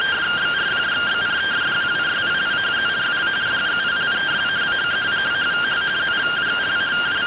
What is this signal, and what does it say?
Two-tone teleprinter signal, about 170 Hz shift.